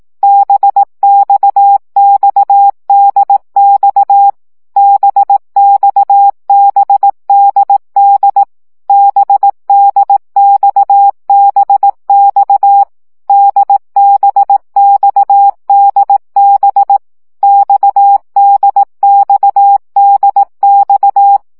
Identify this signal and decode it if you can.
Morse code, heard as a keyed tone, about 18 wpm, 800 Hz. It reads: BXXDX BXBDD BDXBX DBXDB XDXDX
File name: CW gyakorlas d b x 15 baud